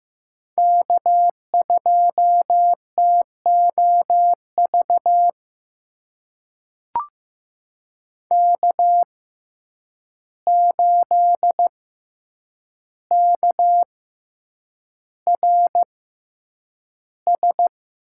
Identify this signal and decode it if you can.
Morse code, heard as a keyed tone, about 15 wpm, spaced as for 8 wpm, 700 Hz. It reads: K2TOV K8KRS
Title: USA Call Signs - 5 Characters - 6x Character Spacing - 15wpm